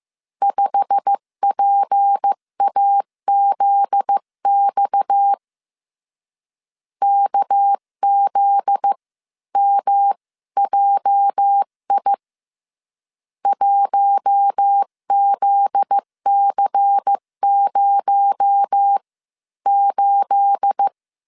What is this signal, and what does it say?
Morse code, heard as a keyed tone, about 15 wpm, spaced as for 13 wpm, 800 Hz. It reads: 5PAZX KZMJI 1ZC08
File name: CW / DET / 1 9 1500C S B 14wpm 3302